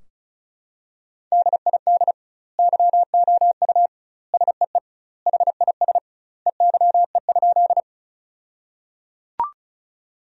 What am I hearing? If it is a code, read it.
Morse code, heard as a keyed tone, about 36 wpm, 700 Hz. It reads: DID YOU SEE HIS EYE?
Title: 2x Word-Spacing - Sentences from Top 300 Words - 35wpm